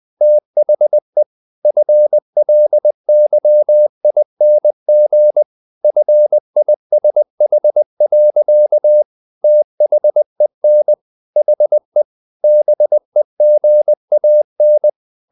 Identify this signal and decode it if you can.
Morse code, heard as a keyed tone, about 20 wpm, 600 Hz. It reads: THE FLYING FISH. THEN HE BEGAN
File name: the old man and the sea / the old man and the sea part 1 20wpm